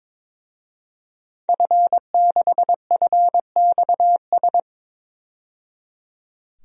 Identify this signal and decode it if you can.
Morse code, heard as a keyed tone, about 22 wpm, 700 Hz. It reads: F6FXS
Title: DXCC-100 Call Signs - 5 Characters - 22wpm